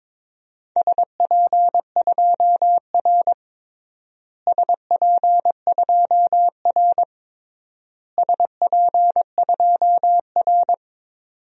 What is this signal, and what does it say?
Morse code, heard as a keyed tone, about 22 wpm, 700 Hz. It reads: SP2R SP2R SP2R
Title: DXCC-100 Call Signs - 4 Characters - Repeated 3x - 22wpm